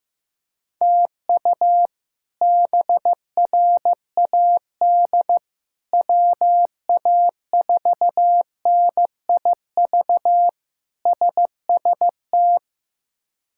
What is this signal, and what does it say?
Morse code, heard as a keyed tone, about 15 wpm, 700 Hz. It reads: TU BRAD WA4NIV SST
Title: K1USN SST QSO Practice - 15wpm